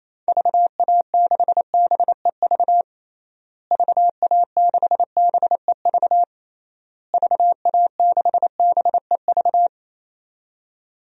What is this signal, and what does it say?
Morse code, heard as a keyed tone, about 27 wpm, 700 Hz. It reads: VA6BEV VA6BEV VA6BEV
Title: DXCC-100 Call Signs - 6 Characters - Repeated 3x - 28wpm